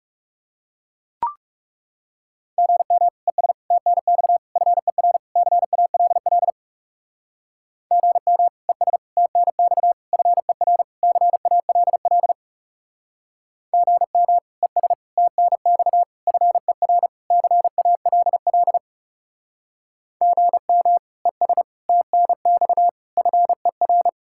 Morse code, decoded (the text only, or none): GM ES TNX FER CALL GM ES TNX FER CALL GM ES TNX FER CALL GM ES TNX FER